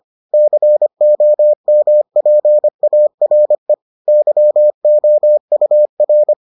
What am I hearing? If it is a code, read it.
Morse code, heard as a keyed tone, about 25 wpm, 600 Hz. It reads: COMPARE YOUR